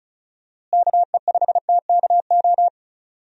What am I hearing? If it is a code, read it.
Morse code, heard as a keyed tone, about 36 wpm, 700 Hz. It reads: KE5TKO
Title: USA Call Signs - 6 Characters - Repeated 2x - 35wpm